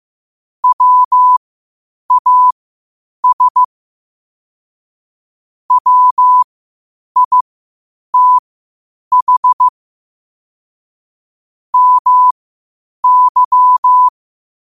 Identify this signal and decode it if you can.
Morse code, heard as a keyed tone, about 15 wpm, spaced as for 8 wpm, 1000 Hz. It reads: WAS WITH MY
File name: qotdfiles / 8wpm